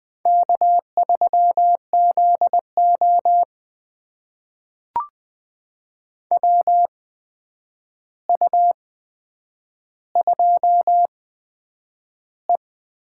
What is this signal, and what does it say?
Morse code, heard as a keyed tone, about 20 wpm, spaced as for 9 wpm, 700 Hz. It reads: K3ZO WU2E
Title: USA Call Signs - 4 Characters - 8x Character Spacing - 20wpm